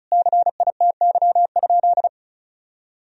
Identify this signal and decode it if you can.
Morse code, heard as a keyed tone, about 34 wpm, 700 Hz. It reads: CITY?
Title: Sentences from Top 1000 Words - Review Part 1 - 35wpm